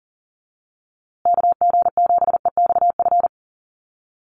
Morse code, heard as a keyed tone, about 40 wpm, 700 Hz. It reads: KG7EXF
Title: USA Call Signs Repeated 3x 40wpm